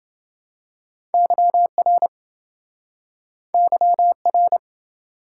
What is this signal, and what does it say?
Morse code, heard as a keyed tone, about 27 wpm, 700 Hz. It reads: YR YR